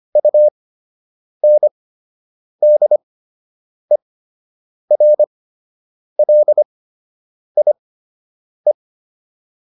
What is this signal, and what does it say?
Morse code, heard as a keyed tone, about 25 wpm, spaced as for 8 wpm, 600 Hz. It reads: UNDERLIE